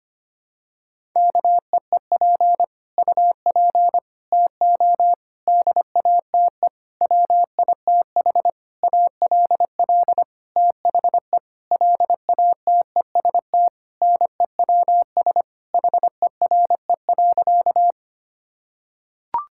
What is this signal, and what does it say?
Morse code, heard as a keyed tone, about 25 wpm, 700 Hz. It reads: KEEP UP TO DATE WITH ALL THE LATEST NEWS HERE.
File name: practice / 25wpm / Sentences with 9-Word Phrases - 25wpm